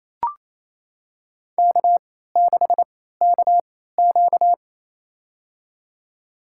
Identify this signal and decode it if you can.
Morse code, heard as a keyed tone, about 27 wpm, spaced as for 16 wpm, 700 Hz. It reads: K6KQ
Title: USA Call Signs - 4 Characters - 3x Character Spacing - 28wpm